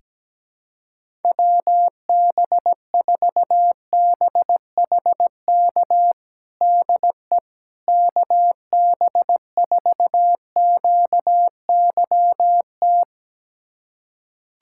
Morse code, heard as a keyed tone, about 17 wpm, 700 Hz. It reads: WB4BHK DE KB4QYT
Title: USA Callsign DE Callsign - 6 Characters - 17wpm